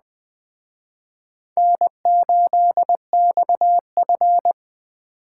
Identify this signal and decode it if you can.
Morse code, heard as a keyed tone, about 21 wpm, 700 Hz. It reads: N8XF